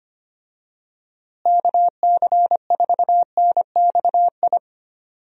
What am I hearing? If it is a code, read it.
Morse code, heard as a keyed tone, about 25 wpm, 700 Hz. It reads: KC4NXI
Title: USA Call Signs - 6 Characters - Repeated 2x - 25wpm